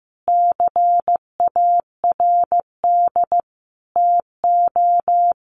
Morse code, heard as a keyed tone, about 15 wpm, 700 Hz. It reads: CARD TO
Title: Sentences from Top 600 Words 15wpm